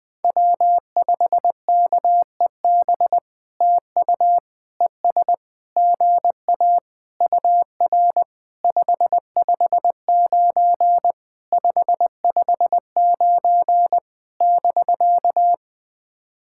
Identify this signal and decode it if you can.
Morse code, heard as a keyed tone, about 20 wpm, 700 Hz. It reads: W5KEB TU ES GA UR 559 559 <BK>
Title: POTA QSO Practice - 20wpm